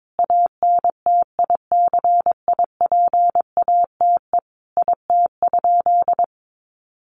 Morse code, heard as a keyed tone, about 22 wpm, 700 Hz. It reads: ANTICIPATE IT?
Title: Sentences from Top 1500 Words 22wpm